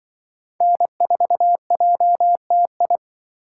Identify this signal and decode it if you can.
Morse code, heard as a keyed tone, about 24 wpm, 700 Hz. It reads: N4JTI
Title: Speed Racing - USA 1x3 Call Signs 20wpm